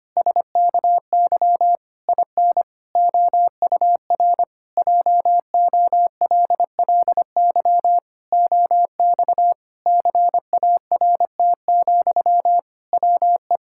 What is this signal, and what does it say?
Morse code, heard as a keyed tone, about 25 wpm, 700 Hz. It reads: SKY IN OUR JOLLY OX CART, WE
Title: Pangram Challenge v2 - 25wpm